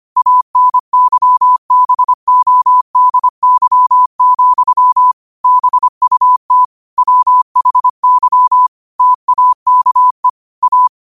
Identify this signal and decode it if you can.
Morse code, heard as a keyed tone, about 25 wpm, 1000 Hz. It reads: ANYBODY, BUT WHY TAKE A
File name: qotdfiles / 25wpm